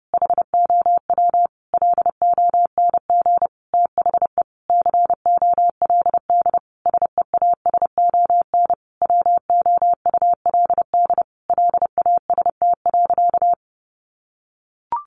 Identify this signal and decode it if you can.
Morse code, heard as a keyed tone, about 30 wpm, 700 Hz. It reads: HOW LONG THE COLD SEASON WOULD LAST.
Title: Sentences from Top 1400 Words 30wpm